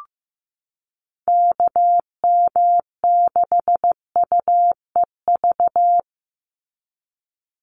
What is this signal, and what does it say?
Morse code, heard as a keyed tone, about 15 wpm, 700 Hz. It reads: KM6UEV
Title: USA Call Signs 15wpm